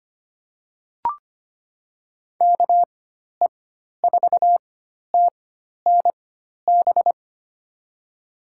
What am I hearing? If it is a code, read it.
Morse code, heard as a keyed tone, about 25 wpm, spaced as for 12 wpm, 700 Hz. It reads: KE4TNB
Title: USA Call Signs - 6 Characters - 4x Character Spacing - 25wpm